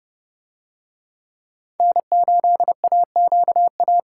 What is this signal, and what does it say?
Morse code, heard as a keyed tone, about 29 wpm, 700 Hz. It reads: N8AQA